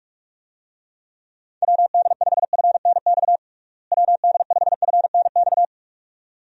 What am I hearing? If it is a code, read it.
Morse code, heard as a keyed tone, about 42 wpm, 700 Hz. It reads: WD5FNX WD5FNX
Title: USA Call Signs - 6 Characters - Repeated 2x - 45wpm